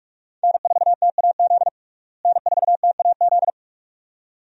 Morse code, heard as a keyed tone, about 45 wpm, 700 Hz. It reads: N4TAZ N4TAZ
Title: USA Call Signs - 5 Characters - Repeated 2x - 45wpm